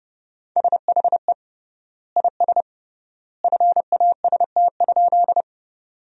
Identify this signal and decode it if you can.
Morse code, heard as a keyed tone, about 30 wpm, 700 Hz. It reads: SHE IS FAST?
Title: 3x Word-Spacing - Sentences from Top 400 Words - 30wpm